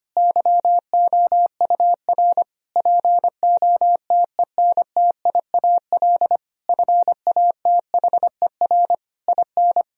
Morse code, heard as a keyed tone, about 25 wpm, 700 Hz. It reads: YOUR POTENTIAL FATHER IN
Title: Sentences from Top 1000 Words - Review Part 4 - 25wpm